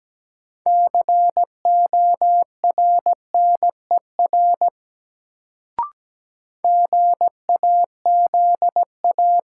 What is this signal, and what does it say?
Morse code, heard as a keyed tone, about 17 wpm, 700 Hz. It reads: CORNER GAZA
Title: Sets of 5 Words - New Word Drill - 17wpm